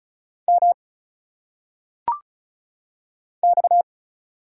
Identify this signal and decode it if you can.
Morse code, heard as a keyed tone, about 36 wpm, 700 Hz. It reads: M X
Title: Single Letters - Phonetic Pronunciation 35wpm